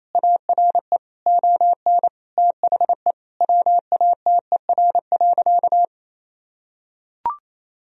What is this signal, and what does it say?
Morse code, heard as a keyed tone, about 28 wpm, 700 Hz. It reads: ARE ON THE WATER.
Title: Sentences from Top 1000 Words - Review Part 1 - 28wpm